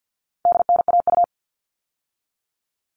Morse code, heard as a keyed tone, about 50 wpm, 700 Hz. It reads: DNAU
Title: All Cut Numbers - 4-Digit 50wpm